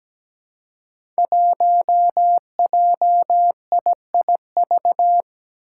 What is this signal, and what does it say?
Morse code, heard as a keyed tone, about 17 wpm, 700 Hz. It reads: 1JIIV
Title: Five Challenging Letter-Number Combinations 17wpm